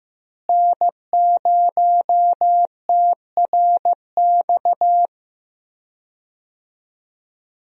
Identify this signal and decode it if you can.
Morse code, heard as a keyed tone, about 15 wpm, 700 Hz. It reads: N0TRX
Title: USA Call Signs - 5 Characters - Repeated 2x - 15wpm